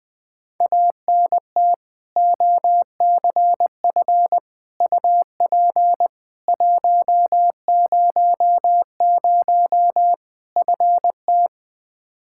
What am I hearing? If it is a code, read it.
Morse code, heard as a keyed tone, about 21 wpm, 700 Hz. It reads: ANT OCF UP 100 FT
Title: Practice QSO - Exchange 2 - Info - 20wpm